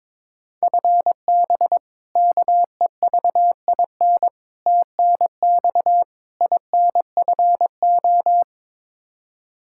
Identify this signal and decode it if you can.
Morse code, heard as a keyed tone, about 22 wpm, 700 Hz. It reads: FB KEVIN TNX INFO
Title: Practice QSO - Exchange 3 - More Info - 22wpm